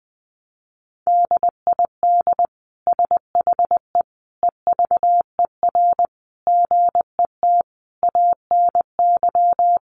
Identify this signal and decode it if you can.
Morse code, heard as a keyed tone, about 20 wpm, 700 Hz. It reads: DID SHE EVER GET ANY
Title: Sentences from Top 1100 Words 20wpm